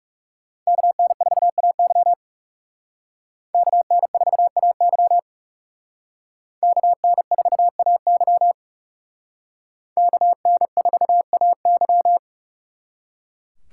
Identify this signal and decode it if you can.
Morse code, heard as a keyed tone, about 36 wpm, 700 Hz. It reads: KN4AY KN4AY KN4AY KN4AY
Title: Speed Racing - USA 2x2 Call Signs 30wpm